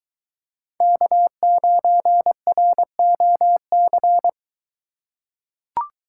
Morse code, heard as a keyed tone, about 23 wpm, 700 Hz. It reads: K9ROC